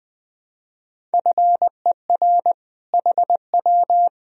Morse code, heard as a keyed tone, about 20 wpm, 700 Hz. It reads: FER HW
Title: Sentences from Top 200 Words and Abbreviations 20wpm